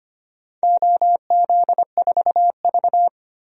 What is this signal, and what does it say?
Morse code, heard as a keyed tone, about 25 wpm, 700 Hz. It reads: OZ4V
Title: DXCC-100 Call Signs - 4 Characters - 25wpm